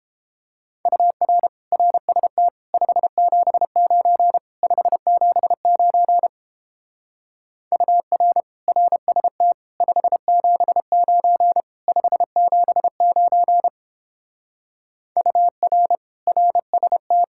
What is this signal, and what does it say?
Morse code, heard as a keyed tone, about 29 wpm, 700 Hz. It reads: UR RST 579 579 UR RST 579 579 UR RST